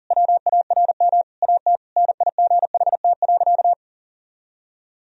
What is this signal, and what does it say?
Morse code, heard as a keyed tone, about 37 wpm, 700 Hz. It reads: WARM AT NIGHT.
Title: Sentences from Top 1000 Words - Review Part 2 - 40wpm